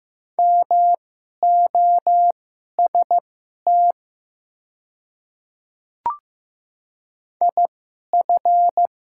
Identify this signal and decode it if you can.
Morse code, heard as a keyed tone, about 15 wpm, spaced as for 11 wpm, 700 Hz. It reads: MOST IF
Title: 2x Character Spacing - Top 300 Words - 15wpm